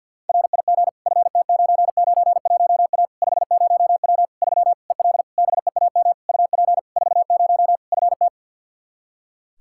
Morse code, heard as a keyed tone, about 53 wpm, 700 Hz. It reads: RIG FT991A 50W 3 EL BEAM UP 40 FT
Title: Practice QSO - Exchange 2 - Info - 50wpm